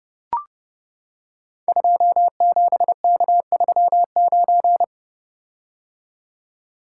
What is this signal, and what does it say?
Morse code, heard as a keyed tone, about 32 wpm, 690 Hz. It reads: 27K39
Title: Code Groups - 5 Letter:Numbers - 30wpm